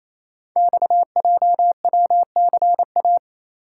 Morse code, heard as a keyed tone, about 28 wpm, 690 Hz. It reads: XJWCA
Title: Code Groups - 5 Letter:Numbers - 28wpm